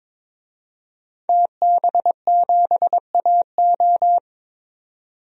Morse code, heard as a keyed tone, about 22 wpm, 700 Hz. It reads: TB7AO
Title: DXCC-100 Call Signs - 5 Characters - 22wpm